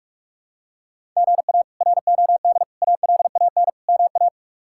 Morse code, heard as a keyed tone, about 46 wpm, 700 Hz. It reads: GA ROD ALAN MA
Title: K1USN SST QSO Practice - 45wpm